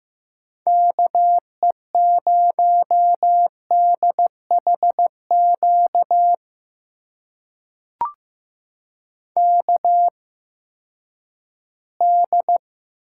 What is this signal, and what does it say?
Morse code, heard as a keyed tone, about 15 wpm, spaced as for 10 wpm, 700 Hz. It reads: KE0DHQ KD
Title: USA Call Signs - 6 Characters - 8x Character Spacing - 15wpm